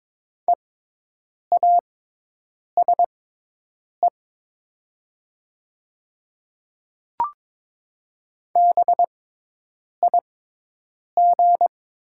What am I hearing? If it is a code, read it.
Morse code, heard as a keyed tone, about 22 wpm, spaced as for 8 wpm, 700 Hz. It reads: EASE BIG
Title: 6x Character Spacing - Top 300 Words - 22wpm